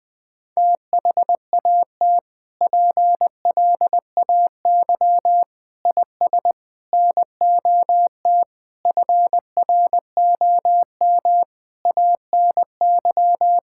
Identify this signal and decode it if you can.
Morse code, heard as a keyed tone, about 21 wpm, 700 Hz. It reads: THAT PLAY IS NOT FROM ANY
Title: Sentences from Top 1000 Words - Review Part 5 - 20wpm